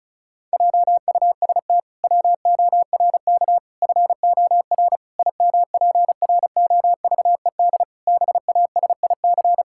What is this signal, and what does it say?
Morse code, heard as a keyed tone, about 34 wpm, 700 Hz. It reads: JUST WORK FOR IMPROVED BASIC
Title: Pangram Challenge v2 - 35wpm